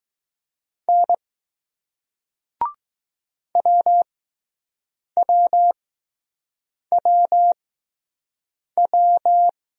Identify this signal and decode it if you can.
Morse code, heard as a keyed tone, about 21 wpm, spaced as for 6 wpm, 700 Hz. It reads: N WWWW